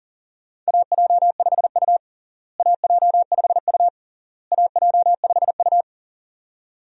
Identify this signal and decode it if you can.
Morse code, heard as a keyed tone, about 40 wpm, 700 Hz. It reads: AJ5U AJ5U AJ5U
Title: USA Call Signs - 4 Characters - Repeated 3x - 40wpm